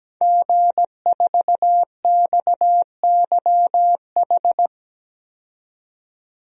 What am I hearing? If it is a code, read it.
Morse code, heard as a keyed tone, about 17 wpm, 700 Hz. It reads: G4XYH
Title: DXCC-100 Call Signs - 5 Characters - 17wpm